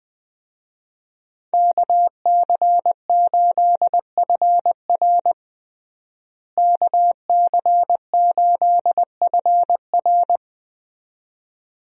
Morse code, heard as a keyed tone, about 21 wpm, 700 Hz. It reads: KC8FR KC8FR
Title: USA Call Signs - 5 Characters - Repeated 2x - 20wpm